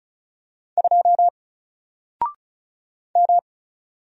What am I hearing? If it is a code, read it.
Morse code, heard as a keyed tone, about 36 wpm, 700 Hz. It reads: 2 M
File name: Single-Letter-Number-35wpm